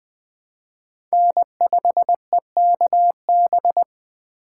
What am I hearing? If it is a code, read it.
Morse code, heard as a keyed tone, about 20 wpm, 700 Hz. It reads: N5EKB